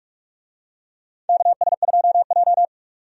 Morse code, heard as a keyed tone, about 44 wpm, 700 Hz. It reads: KS2J
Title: USA Call Signs - 4 Characters - 45wpm